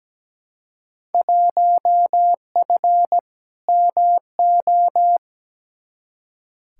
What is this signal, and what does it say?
Morse code, heard as a keyed tone, about 17 wpm, 700 Hz. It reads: 1F MO
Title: Field Day Exchanges Encore - 17wpm